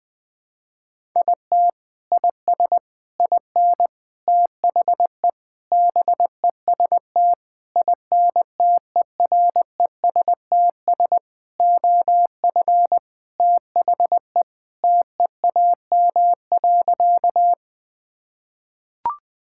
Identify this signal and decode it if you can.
Morse code, heard as a keyed tone, about 20 wpm, 700 Hz. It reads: IT IS IN THE BEST INTERESTS OF THE TEAM.
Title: Sentences with 6-Word Phrases - 20wpm